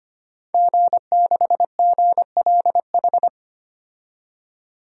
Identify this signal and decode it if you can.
Morse code, heard as a keyed tone, about 25 wpm, 700 Hz. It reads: G6GLH
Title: DXCC-100 Call Signs - 5 Characters - 25wpm